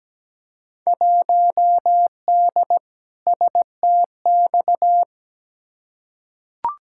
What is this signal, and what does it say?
Morse code, heard as a keyed tone, about 17 wpm, 700 Hz. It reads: 1D STX
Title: Field Day Exchanges - 17wpm